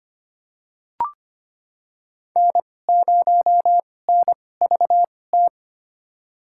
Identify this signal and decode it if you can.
Morse code, heard as a keyed tone, about 25 wpm, spaced as for 18 wpm, 700 Hz. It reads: N0NVT
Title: USA Call Signs - 5 Characters - 2x Character Spacing - 25wpm